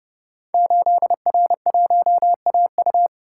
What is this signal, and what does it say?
Morse code, heard as a keyed tone, about 30 wpm, 700 Hz. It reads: 8R1AU
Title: Code Groups - 5 Letter:Numbers - 30wpm